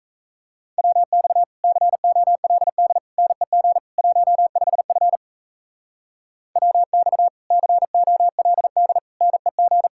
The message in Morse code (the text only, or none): WX COLD NEG 15F WX COLD NEG